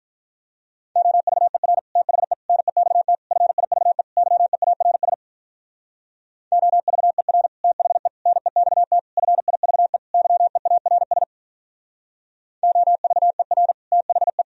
Morse code, heard as a keyed tone, about 46 wpm, 700 Hz. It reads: OVER THE NEXT FIVE YEARS OVER THE NEXT FIVE YEARS OVER THE